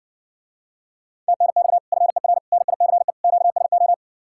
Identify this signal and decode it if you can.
Morse code, heard as a keyed tone, about 60 wpm, 700 Hz. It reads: TNX FER NICE QSO